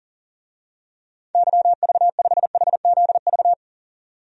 Morse code, heard as a keyed tone, about 37 wpm, 700 Hz. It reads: YV5HZV
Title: DXCC-100 Call Signs - 6 Characters - 40wpm